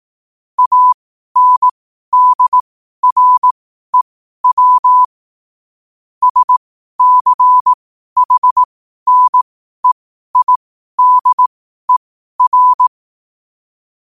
Morse code, heard as a keyed tone, about 18 wpm, spaced as for 12 wpm, 1000 Hz. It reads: ANDREW SCHNEIDER
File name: qotdfiles / 12wpm